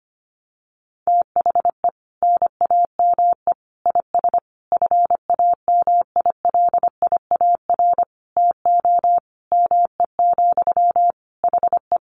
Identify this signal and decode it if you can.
Morse code, heard as a keyed tone, about 25 wpm, 700 Hz. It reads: THE NAME IS FAMILIAR TO ME, HE
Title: Sentences from Top 1000 Words 25wpm